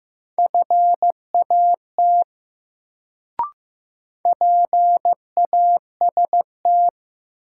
Morse code, heard as a keyed tone, about 15 wpm, 700 Hz. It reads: FAT PAST
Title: Sentences from Top 600 Words - New Word Drill - 15wpm